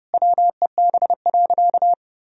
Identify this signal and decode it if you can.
Morse code, heard as a keyed tone, about 32 wpm, 700 Hz. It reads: WEB.
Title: Sentences from Top 1000 Words - Review Part 4 - 30wpm